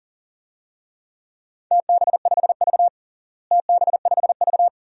Morse code, heard as a keyed tone, about 40 wpm, 700 Hz. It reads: TB5V TB5V
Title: DXCC-100 Call Signs - 4 Characters - Repeated 2x - 40wpm